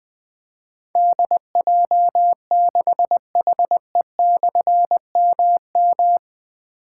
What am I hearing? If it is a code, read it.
Morse code, heard as a keyed tone, about 20 wpm, 700 Hz. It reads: DJ6HE/MM
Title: Difficult Call Signs 20wpm